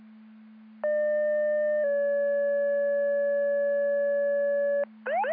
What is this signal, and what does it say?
Two-tone page, heard as a one-second tone, then a three-second tone, about 600 then 570 Hz. It is heard from 0.8 to 4.8 s.
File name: Andover Twp Boro Siren